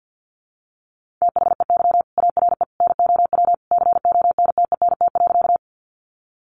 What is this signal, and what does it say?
Morse code, heard as a keyed tone, about 50 wpm, 700 Hz. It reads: THEY ARE NOW CONTENT.
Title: Sentences from Top 900 Words 50wpm